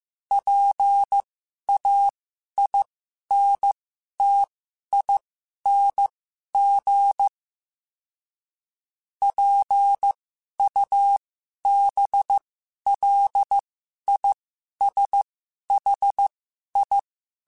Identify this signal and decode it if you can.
Morse code, heard as a keyed tone, about 15 wpm, spaced as for 11 wpm, 800 Hz. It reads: PAINTING PUBLISHI
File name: Special INGs-mp3s / Special INGs-mp3s / Sngl Ing @10